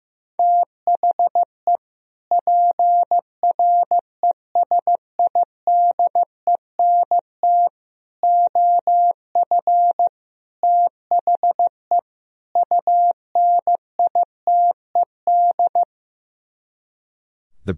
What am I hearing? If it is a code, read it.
Morse code, heard as a keyed tone, about 15 wpm, 700 Hz. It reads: THE PRESIDENT OF THE UNITED
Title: Speed-Racing - Sets of 5 Words 15wpm